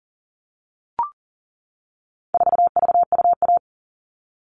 Morse code, heard as a keyed tone, about 40 wpm, 700 Hz. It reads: 4VUA
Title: All Cut Numbers - 4-Digit 40wpm